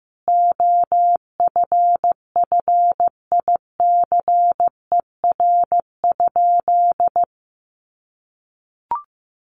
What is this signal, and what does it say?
Morse code, heard as a keyed tone, about 15 wpm, 700 Hz. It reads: OFFICER?
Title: Sentences from Top 1200 Words 15wpm